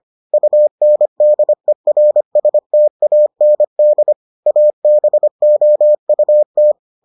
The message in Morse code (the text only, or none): UNDERSTAND ABOUT